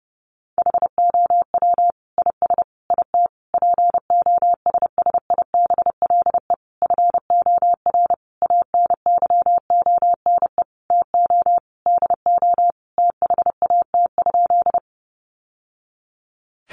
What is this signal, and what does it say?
Morse code, heard as a keyed tone, about 30 wpm, 700 Hz. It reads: HOW IS IT POSSIBLE FOR ANYONE TO DO THAT?
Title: Sentences from Top 900 Words 30wpm